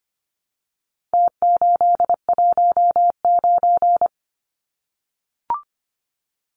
Morse code, heard as a keyed tone, about 25 wpm, 700 Hz. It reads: T819
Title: Numbers - Contest Style - 4-Digit 25wpm